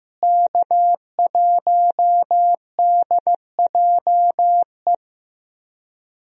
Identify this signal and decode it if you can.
Morse code, heard as a keyed tone, about 15 wpm, 700 Hz. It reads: K1DJE
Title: USA Call Signs - 5 Characters - Repeated 3x - 15wpm